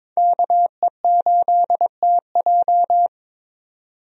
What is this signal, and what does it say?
Morse code, heard as a keyed tone, about 21 wpm, 700 Hz. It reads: KE8TJ
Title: USA Call Signs - 5 Characters - 22wpm